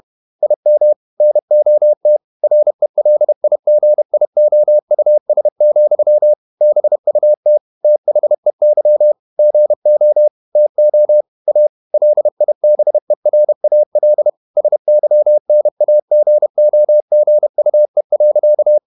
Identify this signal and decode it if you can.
Morse code, heard as a keyed tone, about 31 wpm, 600 Hz. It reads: IM NOT RELIGIOUS, BUT THEY GO TO A LIBERAL SYNAGOGUE.
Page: Never 31wpm | CW med Gnister